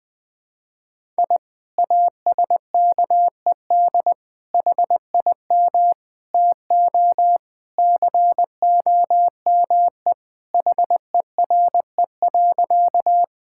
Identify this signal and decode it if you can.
Morse code, heard as a keyed tone, about 20 wpm, 700 Hz. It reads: I ASKED HIM TO COME HERE.
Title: Sentences with 2-Word Phrases - Spoken Beginning - 20wpm